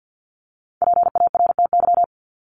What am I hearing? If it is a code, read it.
Morse code, heard as a keyed tone, about 50 wpm, 700 Hz. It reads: PARTY
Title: ES:S Suffix - Top 100 Words with Base Words 50wpm